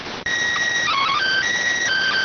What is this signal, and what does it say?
SSTV (slow-scan television), PD 290.